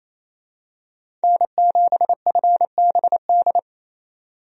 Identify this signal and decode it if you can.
Morse code, heard as a keyed tone, about 28 wpm, 700 Hz. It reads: N7FBD
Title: USA Call Signs - 5 Characters - 28wpm